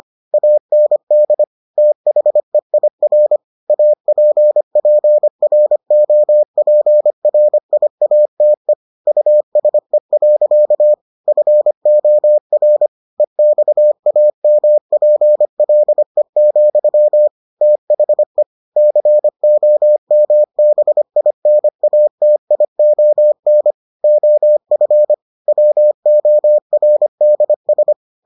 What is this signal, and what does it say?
Morse code, heard as a keyed tone, about 25 wpm, 600 Hz. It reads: AND THEIR APPROPRIATE USE. FOR EXAMPLE, THE COMBINATION OF WORDS